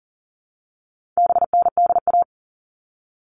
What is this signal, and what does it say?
Morse code, heard as a keyed tone, about 40 wpm, 700 Hz. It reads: BNDA
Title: All Cut Numbers - 4-Digit 40wpm